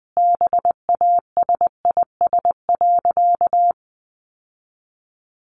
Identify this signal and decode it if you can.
Morse code, heard as a keyed tone, about 20 wpm, 700 Hz. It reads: BASIS.
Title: Sentences from Top 1500 Words 20wpm